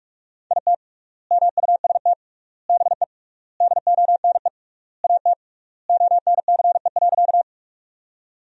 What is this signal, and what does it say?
Morse code, heard as a keyed tone, about 46 wpm, 700 Hz. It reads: IT MUST BE DONE AT ONCE.
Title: 3x Word-Spacing - Sentences from Top 500 Words - 45wpm